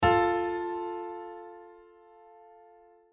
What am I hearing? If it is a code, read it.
Dial tone (North America), 350 + 440 Hz.